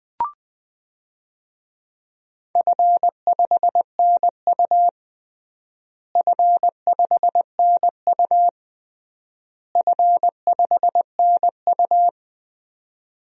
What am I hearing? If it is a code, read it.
Morse code, heard as a keyed tone, about 20 wpm, 700 Hz. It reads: F5NU F5NU F5NU
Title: DXCC-100 Call Signs - 4 Characters - Repeated 3x - 20wpm